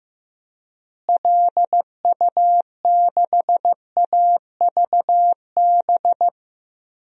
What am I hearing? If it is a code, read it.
Morse code, heard as a keyed tone, about 15 wpm, 700 Hz. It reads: LU6AVB
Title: DXCC-100 Call Signs - 6 Characters - Repeated 3x - 15wpm